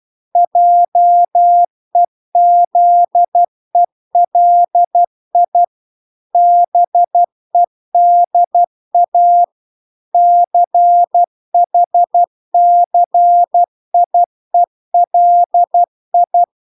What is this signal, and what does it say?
Morse code, heard as a keyed tone, about 12 wpm, 700 Hz. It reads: JEZELI BEDA CHCIELI